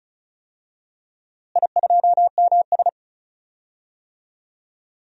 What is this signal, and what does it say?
Morse code, heard as a keyed tone, about 35 wpm, 700 Hz. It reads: I2MS
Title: DXCC-100 Call Signs - 4 Characters - Repeated 2x - 35wpm